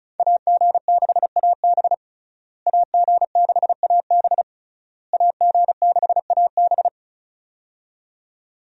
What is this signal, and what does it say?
Morse code, heard as a keyed tone, about 36 wpm, 700 Hz. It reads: AG6AB AG6AB AG6AB
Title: USA Call Signs - 5 Characters - Repeated 3x - 35wpm